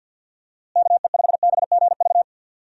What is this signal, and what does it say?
Morse code, heard as a keyed tone, about 47 wpm, 700 Hz. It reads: KE5BGV
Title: USA Call Signs - 6 Characters - Repeated 2x - 50wpm